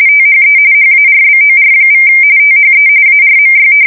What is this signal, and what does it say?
RTTY, about 45.45 baud. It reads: THE HARBOR= THE ARMY
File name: fsk170 / usnavy2